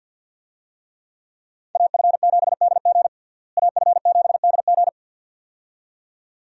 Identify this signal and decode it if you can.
Morse code, heard as a keyed tone, about 49 wpm, 700 Hz. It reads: AF7DG AF7DG
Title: USA Call Signs - 5 Characters - Repeated 2x - 50wpm